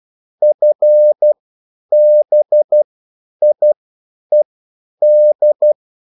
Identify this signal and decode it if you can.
Morse code, heard as a keyed tone, about 12 wpm, spaced as for 9 wpm, 600 Hz. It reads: FBIED